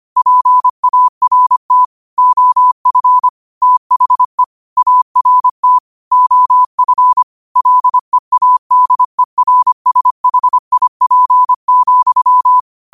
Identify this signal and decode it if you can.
Morse code, heard as a keyed tone, about 25 wpm, 1000 Hz. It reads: PART OF THE ART OF LEADERSHIP,
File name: qotdfiles / 25wpm